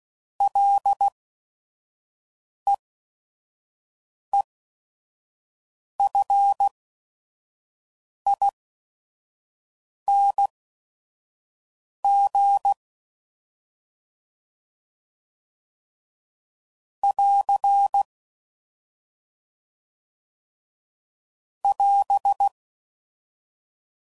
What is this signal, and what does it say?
Morse code, heard as a keyed tone, about 16 wpm, spaced as for 5 wpm, 800 Hz. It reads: LEEFING <AR> <AS>
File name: Prefixes / FLs & THs / Single L's